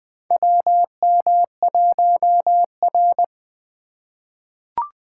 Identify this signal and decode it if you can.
Morse code, heard as a keyed tone, about 21 wpm, 700 Hz. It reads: WM1R
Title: USA Call Signs - 4 Characters - 3x Character Spacing - 20wpm